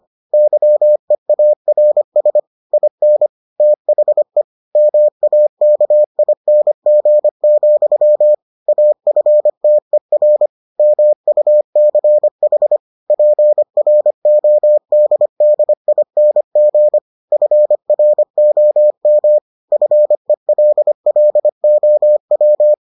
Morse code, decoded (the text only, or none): YEARS IN THE MAKING, AFTER MUCH PRODDING FROM FELLOW